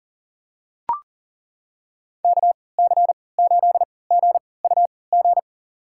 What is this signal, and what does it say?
Morse code, heard as a keyed tone, about 38 wpm, spaced as for 22 wpm, 700 Hz. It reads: KC8GUG
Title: USA Call Signs - 6 Characters - 3x Character Spacing - 40wpm